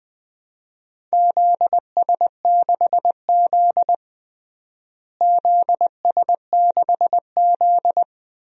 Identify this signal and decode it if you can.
Morse code, heard as a keyed tone, about 20 wpm, 700 Hz. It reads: ZS6Z ZS6Z
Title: DXCC-100 Call Signs - 4 Characters - Repeated 2x - 20wpm